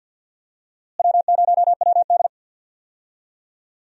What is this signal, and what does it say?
Morse code, heard as a keyed tone, about 48 wpm, 700 Hz. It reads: W0WD
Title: USA Call Signs - 4 Characters - Repeated 2x - 50wpm